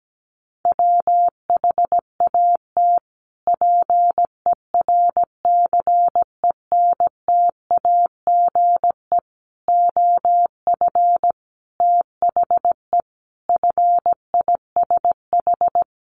Morse code, heard as a keyed tone, about 17 wpm, 700 Hz. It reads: WHAT PERCENTAGE OF THE FISH
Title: Sentences from Top 1300 Words 17wpm